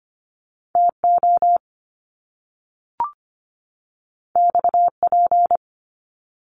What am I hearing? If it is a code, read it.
Morse code, heard as a keyed tone, about 25 wpm, 700 Hz. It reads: TO XP
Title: Top Two Letter Combinations in Words 25wpm